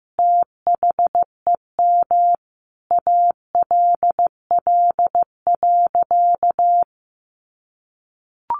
Morse code, heard as a keyed tone, about 15 wpm, 700 Hz. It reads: THEM ALL.
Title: Sentences from Top 1100 Words 15wpm